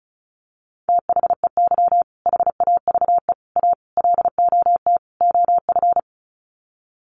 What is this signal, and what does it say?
Morse code, heard as a keyed tone, about 34 wpm, 700 Hz. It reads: THEY HAVE A LOT OF
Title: Sets of 5 Words Encore2 35wpm